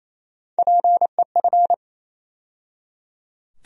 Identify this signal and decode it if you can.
Morse code, heard as a keyed tone, about 28 wpm, 700 Hz. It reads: PEF
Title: USA Call Sign Postfixes - 3 Letters 28wpm